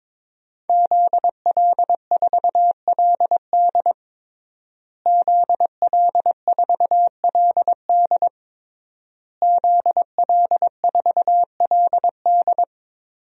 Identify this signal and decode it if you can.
Morse code, heard as a keyed tone, about 21 wpm, 700 Hz. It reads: ZL4LD ZL4LD ZL4LD
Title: DXCC-100 Call Signs - 5 Characters - Repeated 3x - 22wpm